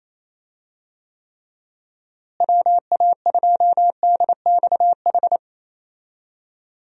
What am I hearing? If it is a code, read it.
Morse code, heard as a keyed tone, about 28 wpm, 700 Hz. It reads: WA2DXH
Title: USA Call Signs - 6 Characters - Repeated 2x - 28wpm